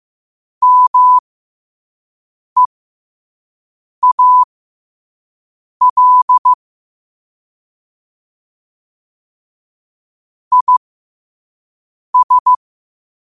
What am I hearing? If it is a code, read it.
Morse code, heard as a keyed tone, about 15 wpm, spaced as for 5 wpm, 1000 Hz. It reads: MEAL IS
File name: 5wpm